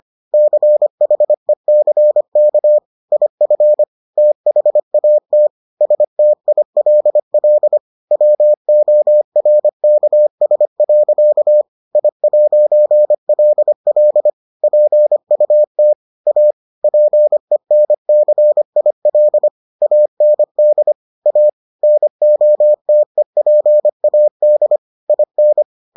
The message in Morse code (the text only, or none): CHECK IF THAT STILL WORKS. I'LL PUT A PENCIL AND A NOTEPAD IN